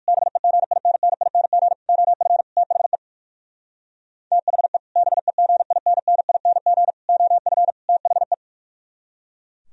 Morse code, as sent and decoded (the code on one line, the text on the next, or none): -... --. .. -. -. .. -. --. / --- ..-. / - .... / - .... / -... --. .. -. -. .. -. --. / --- ..-. / - .... .
BEGINNING OF THE THE BEGINNING OF THE